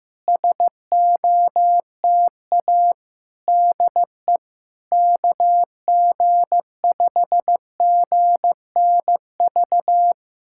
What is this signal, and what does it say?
Morse code, heard as a keyed tone, about 15 wpm, 700 Hz. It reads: SOTA DE KG5GNV
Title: SOTA QSO Practice 15wpm